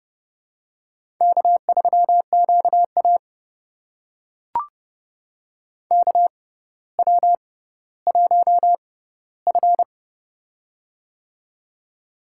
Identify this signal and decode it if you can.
Morse code, heard as a keyed tone, about 30 wpm, spaced as for 15 wpm, 700 Hz. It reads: K3QA KW1F
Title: USA Call Signs - 4 Characters - 6x Character Spacing - 30wpm